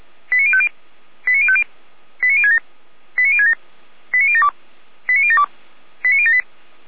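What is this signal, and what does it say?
Five-tone selective call (ZVEI), heard as quick runs of five tones, tones of about 70 ms. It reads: 89050 89050 89067 89067 89072 89072 89078
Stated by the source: ZVEI five-tone call received by SDR.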